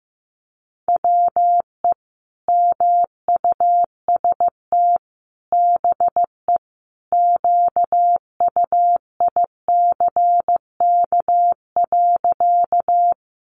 Morse code, heard as a keyed tone, about 15 wpm, 700 Hz. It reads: WE MUST BE QUICK.
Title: Sentences from Top 500 Words 15wpm